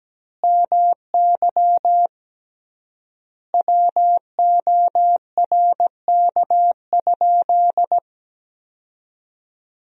Morse code, heard as a keyed tone, about 17 wpm, 700 Hz. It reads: MY WORK?
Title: 3x Word-Spacing - Sentences from Top 500 Words - 17wpm